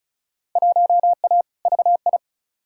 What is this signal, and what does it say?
Morse code, heard as a keyed tone, about 36 wpm, 700 Hz. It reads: 1A VI
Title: Field Day Exchanges Encore - 35wpm